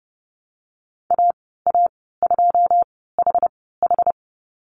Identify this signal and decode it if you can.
Morse code, heard as a keyed tone, about 30 wpm, spaced as for 17 wpm, 700 Hz. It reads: AA2HH
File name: Lesson 14g - Number 2 - Call Signs 30wpm